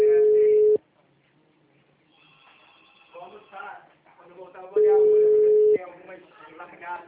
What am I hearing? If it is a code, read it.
Ringback tone (Europe, 425 Hz), with speech.
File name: out- -2020- - - .17593